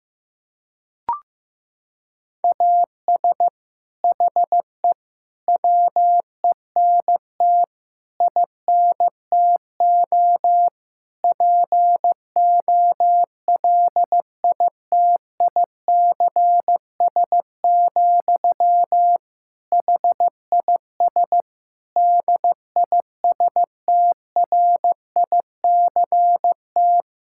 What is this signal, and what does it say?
Morse code, heard as a keyed tone, about 15 wpm, 700 Hz. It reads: AS HE WENT INTO POLITICS, HIS DISTRICT
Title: Sentences from Top 1600 Words 15wpm